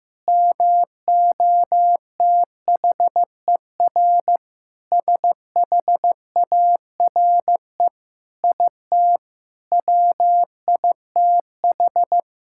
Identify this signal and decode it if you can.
Morse code, heard as a keyed tone, about 15 wpm, 700 Hz. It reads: MOTHER SHARE IT WITH
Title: Sentences from Top 1000 Words - Review Part 6 - 15wpm